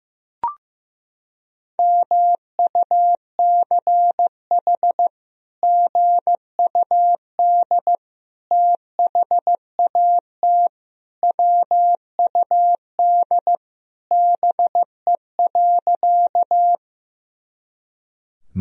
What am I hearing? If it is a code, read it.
Morse code, heard as a keyed tone, about 15 wpm, 700 Hz. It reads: MUCH GUD THAT WUD BE.
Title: Sentences from Top 200 Words and Abbreviations 15wpm